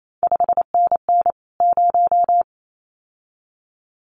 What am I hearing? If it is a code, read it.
Morse code, heard as a keyed tone, about 28 wpm, 700 Hz. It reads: 5NN 0
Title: CQ WW DX Contest Exchange Encore - 28wpm